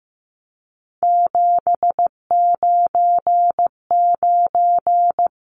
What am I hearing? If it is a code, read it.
Morse code, heard as a keyed tone, about 15 wpm, 700 Hz. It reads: 799
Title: Numbers - 3-Digit 15wpm